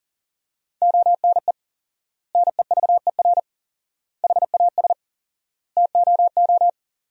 Morse code, heard as a keyed tone, about 40 wpm, 700 Hz. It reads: ONE NEVER HAS TOO
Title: 4x Word-Spacing - Sentences from Top 300 Words - 40wpm